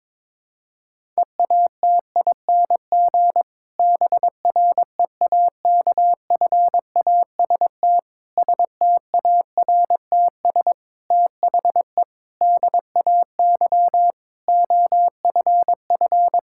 Morse code, heard as a keyed tone, about 22 wpm, 700 Hz. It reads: EATING BREAKFAST STARTS THE DAY OFF
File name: Difficult Sentences with Hints - 22wpm